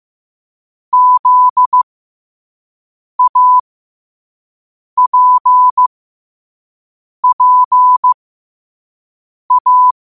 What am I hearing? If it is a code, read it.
Morse code, heard as a keyed tone, about 15 wpm, spaced as for 5 wpm, 1000 Hz. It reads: ZAPPA
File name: qotdfiles / 5wpm